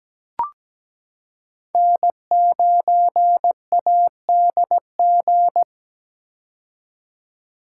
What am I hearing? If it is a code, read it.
Morse code, heard as a keyed tone, about 17 wpm, 700 Hz. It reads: N9ADG
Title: K1USN SST QSO Practice - 17wpm